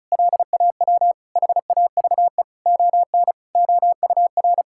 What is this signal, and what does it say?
Morse code, heard as a keyed tone, about 36 wpm, 700 Hz. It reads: LAW HAVE ON OUR
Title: Sentences from Top 1000 Words - Review Part 6 - 35wpm